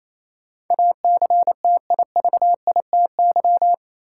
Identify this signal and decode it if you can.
Morse code, heard as a keyed tone, about 28 wpm, 700 Hz. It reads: ACTIVITY
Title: Sentences from Top 1000 Words - Review Part 4 - New Word Drill - 28wpm